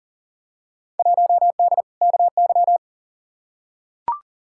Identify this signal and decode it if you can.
Morse code, heard as a keyed tone, about 39 wpm, 700 Hz. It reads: 1D KY
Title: Field Day Exchanges - 40wpm